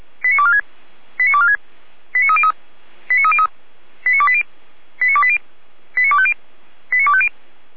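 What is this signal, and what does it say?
Five-tone selective call (ZVEI), heard as quick runs of five tones, tones of about 70 ms. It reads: 89246 89246 89303 89303 89280 89280 89250 89250
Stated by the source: Five-tone ZVEI selective call received on an SDR.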